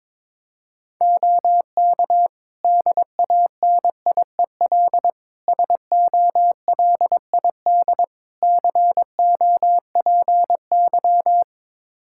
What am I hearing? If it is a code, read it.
Morse code, heard as a keyed tone, about 22 wpm, 700 Hz. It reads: OK DANIEL SOLID COPY
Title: K1USN SST QSO Practice 22wpm